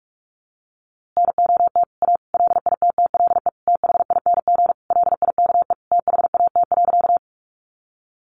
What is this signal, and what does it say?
Morse code, heard as a keyed tone, about 44 wpm, 700 Hz. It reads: NOT A LITTLE THING LIKE THAT.
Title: A Sound Of Thunder 45wpm